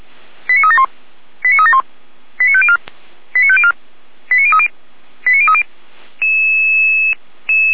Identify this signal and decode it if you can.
Five-tone selective call (ZVEI), heard as quick runs of five tones, tones of about 70 ms. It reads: 89371 89371 89504 89504 89030 89030
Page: Server at audio.f85360.feuerwehr-orlamuende.org